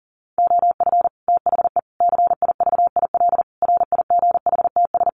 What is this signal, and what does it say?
Morse code, heard as a keyed tone, about 40 wpm, 700 Hz. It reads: OF THE CIVIL RIGHTS
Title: Sets of 5 Words Encore2 40wpm